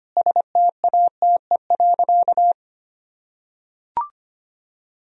Morse code, heard as a keyed tone, about 25 wpm, 700 Hz. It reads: STATE.
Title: Sentences from Top 1000 Words - Review Part 3 - 25wpm